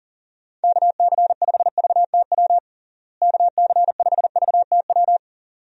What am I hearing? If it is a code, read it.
Morse code, heard as a keyed tone, about 40 wpm, 700 Hz. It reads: KC5VTW KC5VTW
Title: USA Call Signs - 6 Characters - Repeated 2x - 40wpm